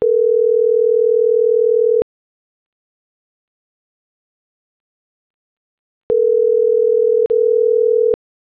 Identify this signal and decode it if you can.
Ringback tone (North America), 440 + 480 Hz.